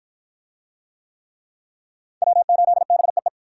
Morse code, heard as a keyed tone, about 53 wpm, 700 Hz. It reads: W8BEE